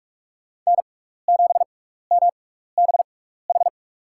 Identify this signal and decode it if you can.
Morse code, heard as a keyed tone, about 44 wpm, spaced as for 15 wpm, 700 Hz. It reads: N7MBH
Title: USA Call Signs - 6x Character Spacing - 45wpm